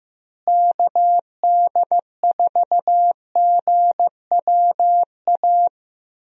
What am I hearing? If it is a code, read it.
Morse code, heard as a keyed tone, about 15 wpm, 700 Hz. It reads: KD4GWA
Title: USA Call Signs - 6 Characters - Repeated 2x - 15wpm